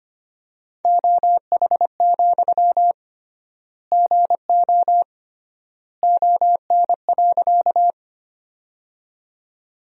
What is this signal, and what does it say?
Morse code, heard as a keyed tone, about 25 wpm, 700 Hz. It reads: OH, GO ON.
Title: 3x Word-Spacing - Sentences from Top 500 Words - 25wpm